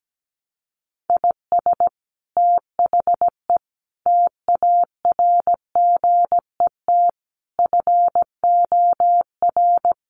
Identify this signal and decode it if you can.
Morse code, heard as a keyed tone, about 17 wpm, 700 Hz. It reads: IS THE TARGET FOR
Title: Sentences from Top 1200 Words 17wpm